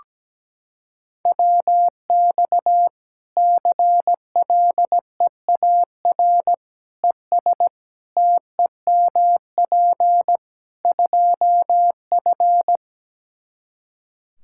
Morse code, heard as a keyed tone, about 17 wpm, 700 Hz. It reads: WX CLEAR ES TEMP 2F
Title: Practice QSO - Exchange 2 - Info - 17wpm